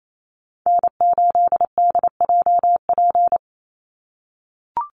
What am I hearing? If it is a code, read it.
Morse code, heard as a keyed tone, about 28 wpm, 700 Hz. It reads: N8DJP
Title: USA Call Signs 28wpm